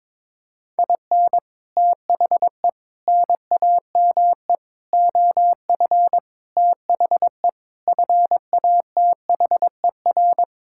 Morse code, heard as a keyed tone, about 22 wpm, 700 Hz. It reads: IN THE NAME OF THE FATHER
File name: Sentences with 9-Word Phrases - 22wpm